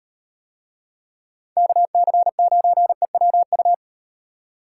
Morse code, heard as a keyed tone, about 38 wpm, 700 Hz. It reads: KC9EWU